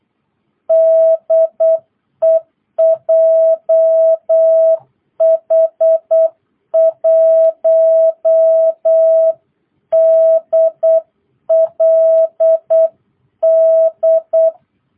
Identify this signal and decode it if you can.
Morse code, heard as a keyed tone, about 8 wpm, 660 Hz. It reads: DE JH1DLD